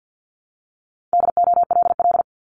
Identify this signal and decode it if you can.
Morse code, heard as a keyed tone, about 50 wpm, 700 Hz. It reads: DOLL
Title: Multi-letter Phonemes 50wpm